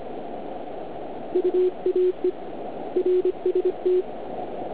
Morse code, heard as a keyed tone, about 25 wpm, 360 Hz. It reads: UAE RST